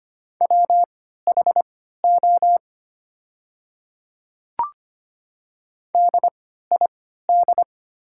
Morse code, heard as a keyed tone, about 25 wpm, spaced as for 14 wpm, 700 Hz. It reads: WHO DID
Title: 3x Character Spacing - Top 100 Words - 25wpm